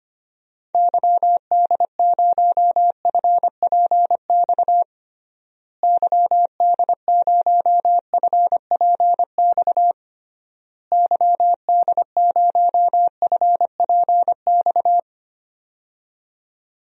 Morse code, heard as a keyed tone, about 25 wpm, 700 Hz. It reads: YD0FPX YD0FPX YD0FPX
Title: DXCC-100 Call Signs - 6 Characters - Repeated 3x - 25wpm